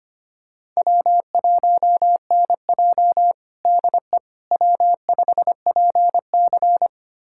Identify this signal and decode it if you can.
Morse code, heard as a keyed tone, about 25 wpm, 700 Hz. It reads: W1NJ DE W5PC
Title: USA Callsign DE Callsign - 4 Characters - 25wpm